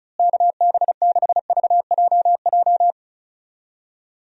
Morse code, heard as a keyed tone, about 36 wpm, 700 Hz. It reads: KB6VJJ
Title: USA Call Signs - 6 Characters - Repeated 3x - 35wpm